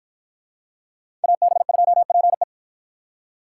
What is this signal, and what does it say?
Morse code, heard as a keyed tone, about 50 wpm, 700 Hz. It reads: AB2PE